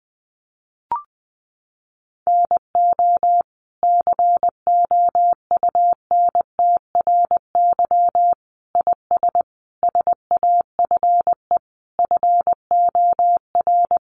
Morse code, heard as a keyed tone, about 20 wpm, 700 Hz. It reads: NO COUNTRY IS SAFE FOR
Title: Sentences from Top 900 Words 20wpm